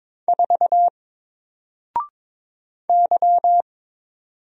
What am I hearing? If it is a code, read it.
Morse code, heard as a keyed tone, about 22 wpm, 700 Hz. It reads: 4 Y
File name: Single-Letter-Number-22wpm